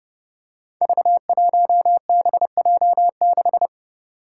Morse code, heard as a keyed tone, about 30 wpm, 700 Hz. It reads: V1BJ6
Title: Five Challenging Letter-Number Combinations 30wpm